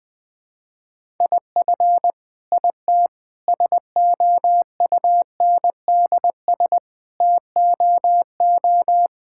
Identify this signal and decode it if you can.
Morse code, heard as a keyed tone, about 20 wpm, 700 Hz. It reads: IF IT SOUNDS TOO
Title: Sets of 9 Words - 20wpm